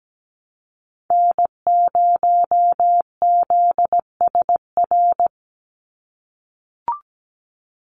Morse code, heard as a keyed tone, about 17 wpm, 700 Hz. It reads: N0ZSR
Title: USA Call Signs 17wpm